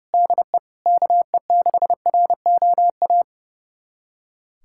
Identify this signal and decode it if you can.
Morse code, heard as a keyed tone, about 30 wpm, 700 Hz. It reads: DE KE6ROA
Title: USA Callsign DE Callsign 30wpm